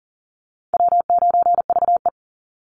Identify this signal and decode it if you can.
Morse code, heard as a keyed tone, about 40 wpm, 700 Hz. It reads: W9VE
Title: Easy Call Signs Repeated 3x - 40wpm